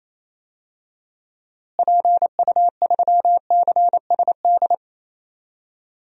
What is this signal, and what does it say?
Morse code, heard as a keyed tone, about 28 wpm, 700 Hz. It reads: PU3CSD
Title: DXCC-100 Call Signs - 6 Characters - Repeated 2x - 28wpm